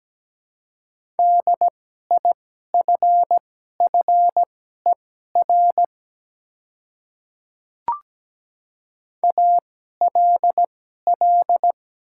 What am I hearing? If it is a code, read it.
Morse code, heard as a keyed tone, about 17 wpm, spaced as for 12 wpm, 700 Hz. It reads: DIFFER ALL
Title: 2x Character Spacing - Top 200 Words - 17wpm